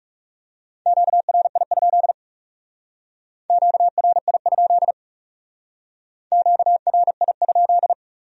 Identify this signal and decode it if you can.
Morse code, heard as a keyed tone, about 40 wpm, 700 Hz. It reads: QRI? QRI? QRI?